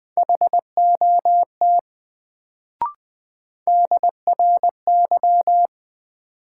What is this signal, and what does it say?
Morse code, heard as a keyed tone, about 20 wpm, 700 Hz. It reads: HOT DRY
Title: Sentences from Top 500 Words - New Word Drill - 20wpm